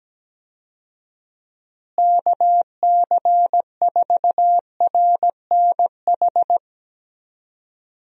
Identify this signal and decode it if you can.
Morse code, heard as a keyed tone, about 17 wpm, 700 Hz. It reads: KC4RNH
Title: USA Call Signs - 6 Characters - Repeated 3x - 17wpm